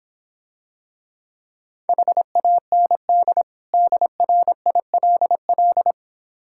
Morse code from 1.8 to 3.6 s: HAND

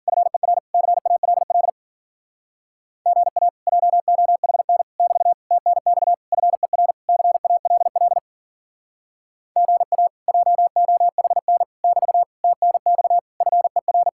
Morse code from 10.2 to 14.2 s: JOHN <BT> TNX FER

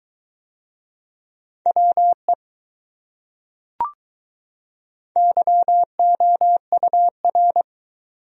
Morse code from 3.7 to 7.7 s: YOUR

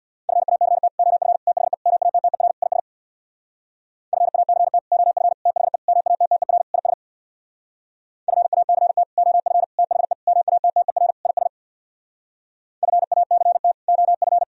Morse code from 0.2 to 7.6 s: FACT OF THE MATTER IS FACT OF THE MATTER IS

